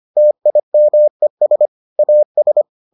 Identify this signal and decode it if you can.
Morse code, heard as a keyed tone, about 25 wpm, 600 Hz. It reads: TIMES AS